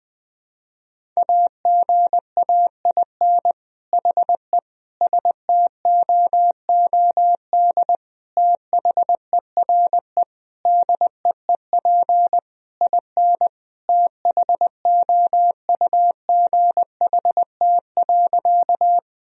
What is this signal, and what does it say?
Morse code, heard as a keyed tone, about 21 wpm, 700 Hz. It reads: AGAIN HE STOOD THERE DEEP IN THOUGHT.
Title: Sentences from Top 1000 Words - Review Part 3 - 20wpm